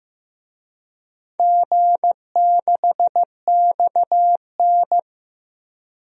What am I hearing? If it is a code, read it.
Morse code, heard as a keyed tone, about 15 wpm, 700 Hz. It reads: G6XN
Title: DXCC-100 Call Signs - 4 Characters - Repeated 2x - 15wpm